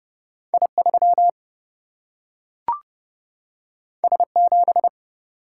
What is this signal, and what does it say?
Morse code, heard as a keyed tone, about 30 wpm, 700 Hz. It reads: I3 S7
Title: Single-Letter Single-Number 30wpm